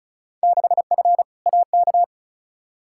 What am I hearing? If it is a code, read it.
Morse code, heard as a keyed tone, about 36 wpm, 700 Hz. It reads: 6F AK
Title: Field Day Exchanges Encore - 35wpm